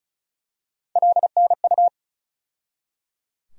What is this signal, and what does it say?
Morse code, heard as a keyed tone, about 36 wpm, 700 Hz. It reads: LNU
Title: USA Call Sign Postfixes - 3 Letters 35wpm